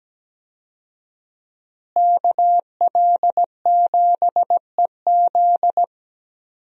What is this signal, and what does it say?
Morse code, heard as a keyed tone, about 17 wpm, 700 Hz. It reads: KL7EZ
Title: DXCC-100 Call Signs - 5 Characters - Repeated 3x - 17wpm